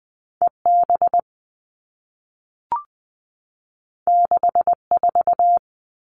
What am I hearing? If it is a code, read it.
Morse code, heard as a keyed tone, about 20 wpm, 700 Hz. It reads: EB 64
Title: All Cut Numbers - 2-Digit 20wpm